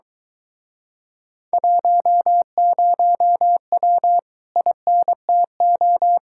Morse code, heard as a keyed tone, about 24 wpm, 700 Hz. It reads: 10W INTO